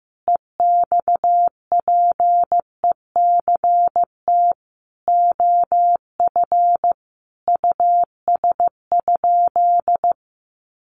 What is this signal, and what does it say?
Morse code, heard as a keyed tone, about 14 wpm, 700 Hz. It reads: EXPECT OF US?
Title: Sentences from Top 1400 Words 15wpm